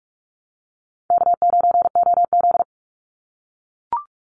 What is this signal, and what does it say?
Morse code, heard as a keyed tone, about 44 wpm, 700 Hz. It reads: K9OZ
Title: Easy Call Signs - 45wpm